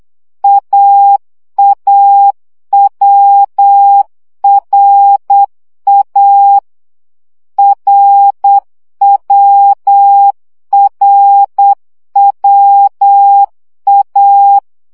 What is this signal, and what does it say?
Morse code, heard as a keyed tone, about 8 wpm, 800 Hz. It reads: AAWRA RWRWA
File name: CW gyakorlas a r w 7 baud